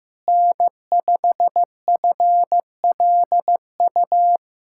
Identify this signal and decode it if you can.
Morse code, heard as a keyed tone, about 15 wpm, 700 Hz. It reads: N5FLU
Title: USA Callsign DE Callsign - 5 Characters - 15wpm